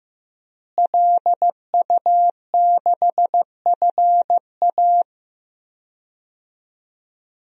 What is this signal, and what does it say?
Morse code, heard as a keyed tone, about 15 wpm, 700 Hz. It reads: LU6FA
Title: DXCC-100 Call Signs - 5 Characters - Repeated 2x - 15wpm